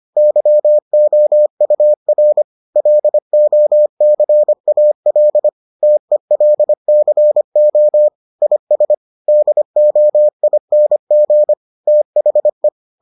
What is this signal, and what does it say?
Morse code, heard as a keyed tone, about 25 wpm, 600 Hz. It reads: YOUR LOCAL TELCO IS DOING THE